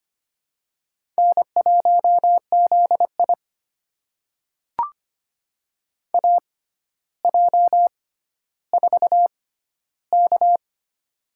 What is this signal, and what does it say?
Morse code, heard as a keyed tone, about 25 wpm, spaced as for 13 wpm, 700 Hz. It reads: N1ZI AJ4K
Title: USA Call Signs - 4 Characters - 6x Character Spacing - 25wpm